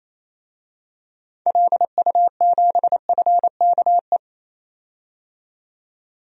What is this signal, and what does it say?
Morse code, heard as a keyed tone, about 28 wpm, 700 Hz. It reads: LU7FKE
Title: DXCC-100 Call Signs - 6 Characters - 28wpm